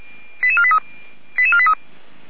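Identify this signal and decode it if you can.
Five-tone selective call (ZVEI), heard as quick runs of five tones, tones of about 70 ms. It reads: 88483 88483
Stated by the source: ZVEI five-tone selective call 08841 received by SDR.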